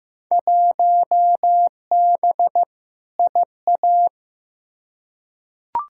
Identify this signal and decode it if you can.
Morse code, heard as a keyed tone, about 15 wpm, 700 Hz. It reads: 1B IA
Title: Field Day Exchanges Encore - 15wpm